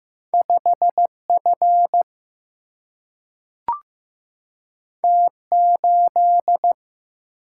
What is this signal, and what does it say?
Morse code, heard as a keyed tone, about 15 wpm, 700 Hz. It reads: 5 F T 8
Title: Single-Letter Single-Number 15wpm